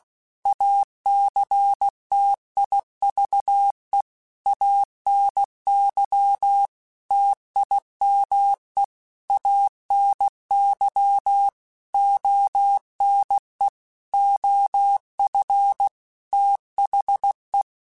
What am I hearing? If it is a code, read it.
Morse code, heard as a keyed tone, about 16 wpm, 800 Hz. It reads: ACTIVE ANY TIME ANY ONE OF THE
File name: Technical Subjects / Technical Subjects-mp3s / OR Gate @15